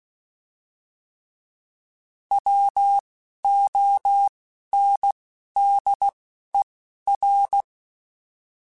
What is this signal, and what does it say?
Morse code, heard as a keyed tone, about 16 wpm, 800 Hz. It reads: W O N D E R
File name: Alphabetical / Alphabetical mp3s / Ws @10